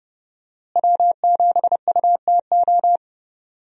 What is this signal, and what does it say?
Morse code, heard as a keyed tone, about 30 wpm, 700 Hz. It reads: W7UTO
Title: USA Call Signs - 5 Characters - Repeated 3x - 30wpm